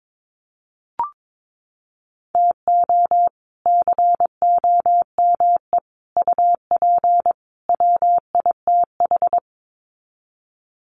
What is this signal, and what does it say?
Morse code, heard as a keyed tone, about 22 wpm, 700 Hz. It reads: TO COME UP WITH
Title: Sets of 4 Words 22wpm